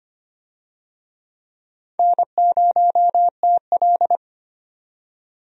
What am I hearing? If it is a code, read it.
Morse code, heard as a keyed tone, about 25 wpm, 700 Hz. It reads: N0TL
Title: USA Call Signs - 4 Characters - 25wpm